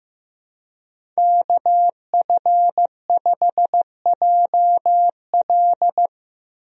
Morse code, heard as a keyed tone, about 15 wpm, 700 Hz. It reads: KF5JL
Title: USA Call Signs - 5 Characters - 15wpm